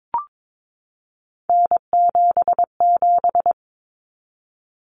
Morse code, heard as a keyed tone, about 22 wpm, 700 Hz. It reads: N77
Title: Common Cut Numbers - 3-Digit 22wpm